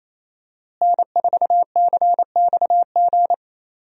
Morse code, heard as a keyed tone, about 28 wpm, 700 Hz. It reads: N4CXG
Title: USA Call Signs - 5 Characters - Repeated 3x - 28wpm